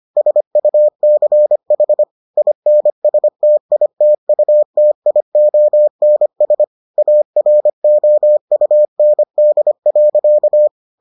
Morse code, heard as a keyed tone, about 25 wpm, 600 Hz. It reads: SUCH INSTITUTIONS AROUND.